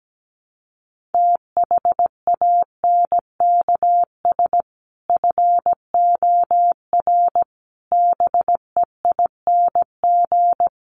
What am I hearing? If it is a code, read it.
Morse code, heard as a keyed tone, about 17 wpm, 700 Hz. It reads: THANKS FOR BEING
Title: Sets of 5 Words 17wpm